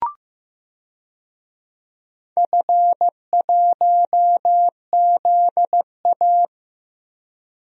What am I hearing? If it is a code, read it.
Morse code, heard as a keyed tone, about 15 wpm, 700 Hz. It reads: F1ZA
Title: DXCC-100 Call Signs - 4 Characters - 15wpm